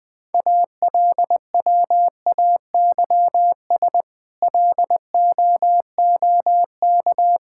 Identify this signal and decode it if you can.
Morse code, heard as a keyed tone, about 20 wpm, 700 Hz. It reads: ALWAYS LOOK
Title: Sets of 8 Words - 20wpm